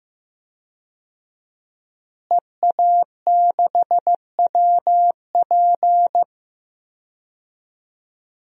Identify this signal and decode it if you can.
Morse code, heard as a keyed tone, about 15 wpm, 700 Hz. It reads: EA6WP
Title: DXCC-100 Call Signs - 5 Characters - Repeated 3x - 15wpm